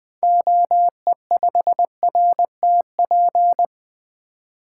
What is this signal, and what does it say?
Morse code, heard as a keyed tone, about 20 wpm, 700 Hz. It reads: OE5RTP
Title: 2023 SOTA Activator Call Signs - 20wpm